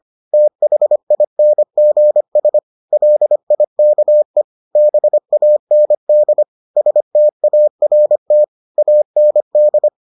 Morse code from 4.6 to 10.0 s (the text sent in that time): BAND START AND